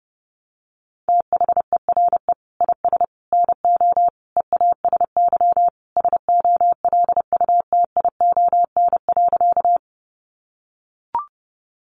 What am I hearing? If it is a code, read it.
Morse code, heard as a keyed tone, about 30 wpm, 700 Hz. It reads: THERE IS NO EASY SOLUTION.
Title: Sentences from Top 1300 Words 30wpm